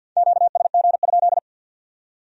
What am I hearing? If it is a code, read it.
Morse code, heard as a keyed tone, about 61 wpm, 700 Hz. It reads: QSG?
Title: Speed-Racing Q-Codes - 50wpm